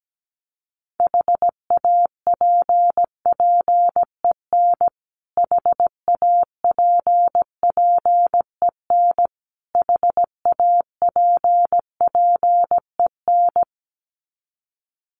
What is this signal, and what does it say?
Morse code, heard as a keyed tone, about 17 wpm, 700 Hz. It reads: HAPPEN HAPPEN HAPPEN
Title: Top 500 Words Repeated 17wpm